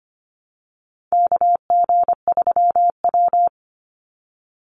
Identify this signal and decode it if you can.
Morse code, heard as a keyed tone, about 25 wpm, 700 Hz. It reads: KG3W
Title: NA SOTA Chaser Callsigns 25wpm